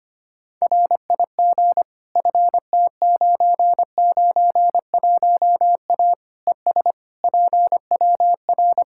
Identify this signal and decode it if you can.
Morse code, heard as a keyed tone, about 25 wpm, 700 Hz. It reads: RIG FT991A ES PWR
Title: Practice QSO - Exchange 2 - Info - 25wpm